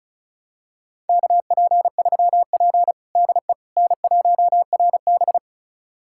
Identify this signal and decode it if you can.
Morse code, heard as a keyed tone, about 36 wpm, 700 Hz. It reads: KP3P DE N1RB
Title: USA Callsign DE Callsign - 4 Characters - 35wpm